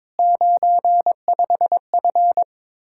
Morse code, heard as a keyed tone, about 22 wpm, 700 Hz. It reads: 95F
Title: Speed-Racing Common QSO Phrases Encore - 22wpm